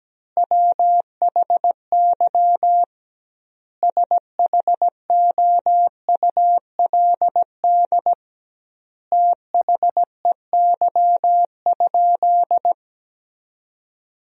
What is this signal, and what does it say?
Morse code, heard as a keyed tone, about 17 wpm, 700 Hz. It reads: WHY SHOULD THEY?
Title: 2x Word-Spacing - Sentences from Top 300 Words - 17wpm